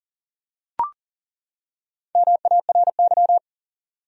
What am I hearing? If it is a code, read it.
Morse code, heard as a keyed tone, about 40 wpm, 700 Hz. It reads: MARY
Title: USA Common Names - Max 4 Letters - 40wpm